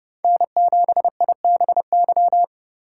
Morse code, heard as a keyed tone, about 28 wpm, 700 Hz. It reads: N7IBY